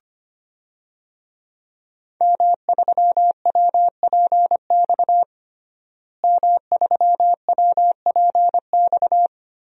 Morse code, heard as a keyed tone, about 25 wpm, 700 Hz. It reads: M3WPX M3WPX
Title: DXCC-100 Call Signs - 5 Characters - Repeated 3x - 25wpm